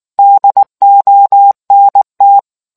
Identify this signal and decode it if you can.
Morse code, heard as a keyed tone, about 19 wpm, 800 Hz. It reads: DONT